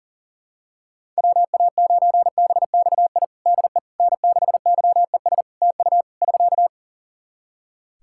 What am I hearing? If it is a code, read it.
Morse code, heard as a keyed tone, about 40 wpm, 700 Hz. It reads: WA9BXI DE N6YES TU <SK>
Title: Practice QSO - Exchange 4 - Ending - 40wpm